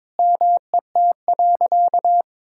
Morse code, heard as a keyed tone, about 22 wpm, 700 Hz. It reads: MET.
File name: Sentences with 7-Word Phrases - 22wpm